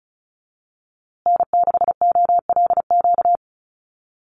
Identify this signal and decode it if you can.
Morse code, heard as a keyed tone, about 34 wpm, 700 Hz. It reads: N6OLQ
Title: USA Call Signs Repeated 3x 35wpm